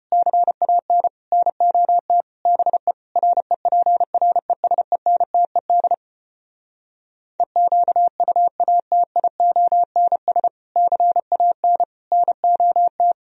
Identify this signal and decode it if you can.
Morse code, heard as a keyed tone, about 33 wpm, 700 Hz. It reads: CAN NOT BE REPRESENTED EQUATIONS CAN NOT